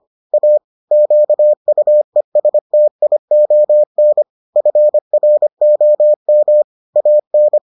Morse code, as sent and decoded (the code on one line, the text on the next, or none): .- / --.- ..- . ... - .. --- -. / ..-. .-. --- -- / .- -.
A QUESTION FROM AN